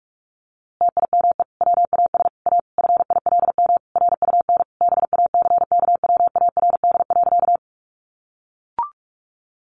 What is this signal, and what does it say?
Morse code, heard as a keyed tone, about 44 wpm, 700 Hz. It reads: TIME WAS A FILM RUN BACKWARD.
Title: A Sound Of Thunder 45wpm